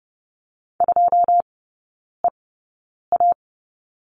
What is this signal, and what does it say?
Morse code, heard as a keyed tone, about 30 wpm, spaced as for 9 wpm, 700 Hz. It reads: 2EA
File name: 081 Lesson 14e - Number 2 - Three Characters 30wpm